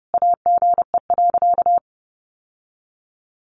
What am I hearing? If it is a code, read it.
Morse code, heard as a keyed tone, about 30 wpm, 700 Hz. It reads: AGE.
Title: Sentences from Top 1100 Words 30wpm